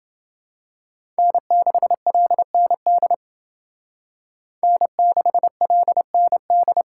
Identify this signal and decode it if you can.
Morse code, heard as a keyed tone, about 29 wpm, 700 Hz. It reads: N6LND N6LND